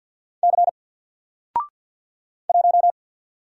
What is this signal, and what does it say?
Morse code, heard as a keyed tone, about 49 wpm, 700 Hz. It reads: C 1
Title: Single Character (Letter:Number:Punctuation:Prosign) Alt 50wpm